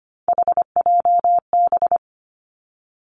Morse code, heard as a keyed tone, about 25 wpm, 700 Hz. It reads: HJB
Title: Three Challenging Letter-Number Combinations 25wpm